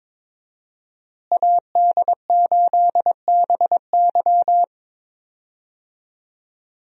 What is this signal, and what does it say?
Morse code, heard as a keyed tone, about 21 wpm, 700 Hz. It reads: AD8BY
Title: USA Call Signs - 5 Characters - Repeated 2x - 22wpm